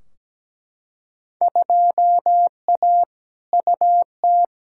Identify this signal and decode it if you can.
Morse code, heard as a keyed tone, about 17 wpm, 700 Hz. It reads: 2A UT
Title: Field Day Exchanges - 17wpm